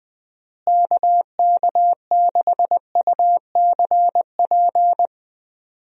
Morse code, heard as a keyed tone, about 21 wpm, 700 Hz. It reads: KK6UCP
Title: USA Call Signs - 6 Characters - Repeated 2x - 20wpm